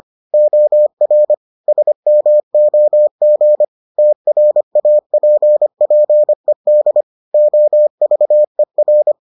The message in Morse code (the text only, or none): OR SMOG TRAPPED OVER